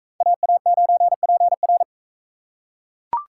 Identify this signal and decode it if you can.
Morse code, heard as a keyed tone, about 41 wpm, 700 Hz. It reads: AA9PR